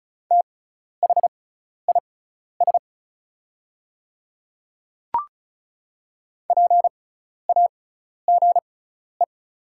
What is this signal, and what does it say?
Morse code, heard as a keyed tone, about 36 wpm, spaced as for 12 wpm, 690 Hz. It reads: THIS PAGE
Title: 6x Character Spacing - Top 300 Words - 35wpm